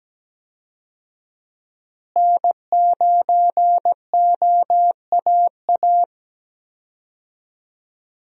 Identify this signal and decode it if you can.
Morse code, heard as a keyed tone, about 17 wpm, 700 Hz. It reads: N9OAA
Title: USA Call Signs - 5 Characters - Repeated 2x - 17wpm